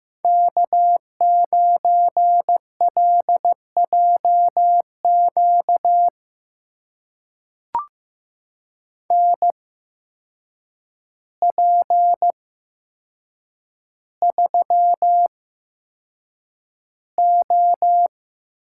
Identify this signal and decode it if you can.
Morse code, heard as a keyed tone, about 15 wpm, spaced as for 7 wpm, 700 Hz. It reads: K9LJQ NP3O
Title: USA Call Signs - 5 Characters - 8x Character Spacing - 15wpm